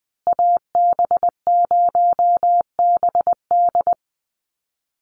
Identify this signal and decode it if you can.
Morse code, heard as a keyed tone, about 20 wpm, 700 Hz. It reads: AB0BD
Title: USA Call Signs Repeated 2x 20wpm